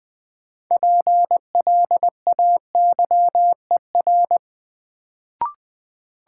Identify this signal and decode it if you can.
Morse code, heard as a keyed tone, about 20 wpm, 700 Hz. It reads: PLAYER
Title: Sentences from Top 1000 Words - Review Part 4 - New Word Drill - 20wpm